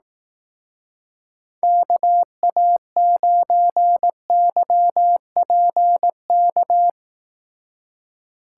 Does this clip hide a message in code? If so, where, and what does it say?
Yes: Morse code from 1.6 to 6.9 s, heard as a keyed tone, about 20 wpm, 700 Hz. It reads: KA9YPK